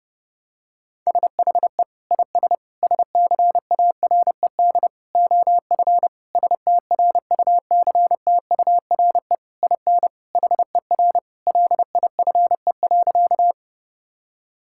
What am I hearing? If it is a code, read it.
Morse code, heard as a keyed tone, about 30 wpm, 700 Hz. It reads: SHE IS SCARED OF STRUCTURE IN HER LIFE.
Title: Sentences from Top 1000 Words - Review Part 6 - 30wpm